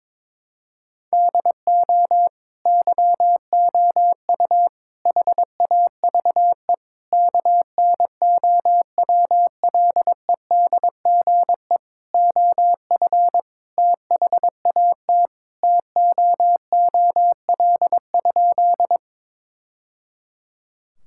Morse code, heard as a keyed tone, about 22 wpm, 700 Hz. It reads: DO YOU HAVE KNOWLEDGE OF THAT TOOL?
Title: Sentences from Top 1000 Words - Review Part 6 - 22wpm